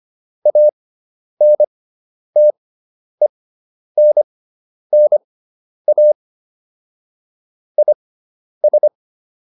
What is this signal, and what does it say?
Morse code, heard as a keyed tone, about 25 wpm, spaced as for 10 wpm, 600 Hz. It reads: ANTENNA IS